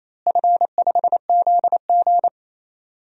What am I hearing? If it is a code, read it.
Morse code, heard as a keyed tone, about 27 wpm, 700 Hz. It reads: F5ZG
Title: DXCC-100 Call Signs - 4 Characters - Repeated 2x - 28wpm